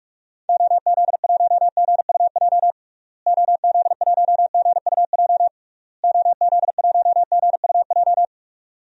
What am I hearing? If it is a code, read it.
Morse code, heard as a keyed tone, about 46 wpm, 700 Hz. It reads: OZ1GUJ OZ1GUJ OZ1GUJ
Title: DXCC-100 Call Signs - 6 Characters - Repeated 3x - 45wpm